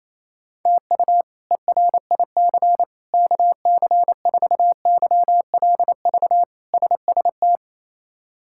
Morse code, heard as a keyed tone, about 28 wpm, 700 Hz. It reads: TU ERIC KC4YLV SST
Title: K1USN SST QSO Practice - 28wpm